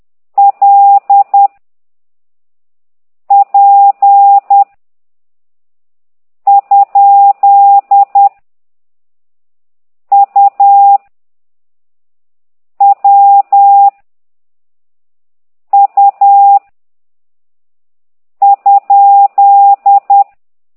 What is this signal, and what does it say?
Morse code, heard as a keyed tone, about 10 wpm, 800 Hz. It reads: L P ? U W U ?
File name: ct2idw / exerc8